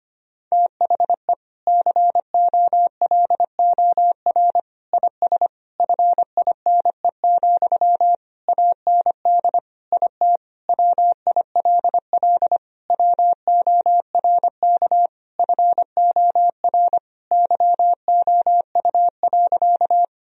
Morse code, heard as a keyed tone, about 25 wpm, 700 Hz. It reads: THE COLOR IS FINE, AND IT WILL WORK FOR YOU.
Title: Sentences from Top 1000 Words - Review Part 2 - 25wpm